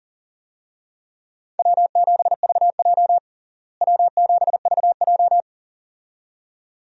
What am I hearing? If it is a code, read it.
Morse code, heard as a keyed tone, about 40 wpm, 700 Hz. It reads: W7VJ W7VJ
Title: USA Call Signs - 4 Characters - Repeated 2x - 40wpm